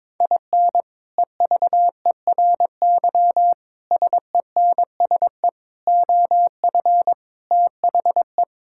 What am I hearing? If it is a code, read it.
Morse code, heard as a keyed tone, about 21 wpm, 700 Hz. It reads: IN EVERY SENSE OF THE
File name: practice / 22wpm / Sets of 6 Words - 22wpm